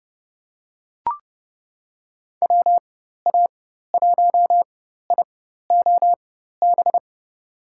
Morse code, heard as a keyed tone, about 32 wpm, spaced as for 14 wpm, 700 Hz. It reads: WA1IOB
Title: USA Call Signs - 4x Character Spacing - 30wpm